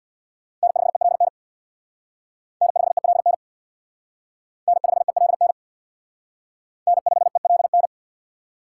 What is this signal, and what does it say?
Morse code, heard as a keyed tone, about 62 wpm, 700 Hz. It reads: N5ELN N5ELN N5ELN N5ELN
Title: Speed Racing - USA 1x3 Call Signs 50wpm